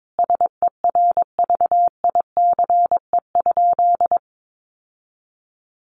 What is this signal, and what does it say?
Morse code, heard as a keyed tone, about 22 wpm, 700 Hz. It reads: SERVICE?
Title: Sentences from Top 1500 Words 22wpm